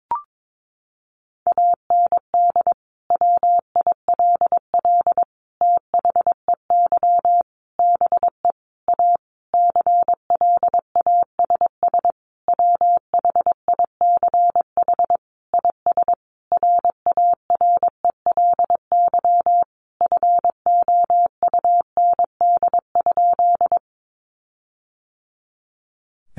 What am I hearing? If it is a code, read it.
Morse code, heard as a keyed tone, about 22 wpm, 700 Hz. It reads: AND WILL THEY BE A CLASS WHICH IS RARELY FOUND?
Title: Sentences from Top 1400 Words 22wpm